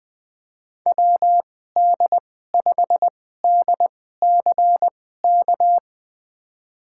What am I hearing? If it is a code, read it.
Morse code, heard as a keyed tone, about 21 wpm, spaced as for 15 wpm, 700 Hz. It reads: WD5DCK
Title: USA Call Signs - 6 Characters - 2x Character Spacing - 20wpm